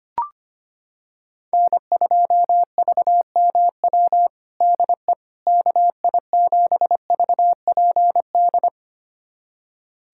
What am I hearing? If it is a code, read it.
Morse code, heard as a keyed tone, about 25 wpm, 700 Hz. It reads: N2VMW DE KI7VPD
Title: USA Callsign DE Callsign 25wpm